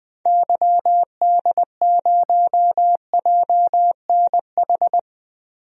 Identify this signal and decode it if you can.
Morse code, heard as a keyed tone, about 20 wpm, 700 Hz. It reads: YD0JNH
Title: DXCC-100 Call Signs - 6 Characters - 20wpm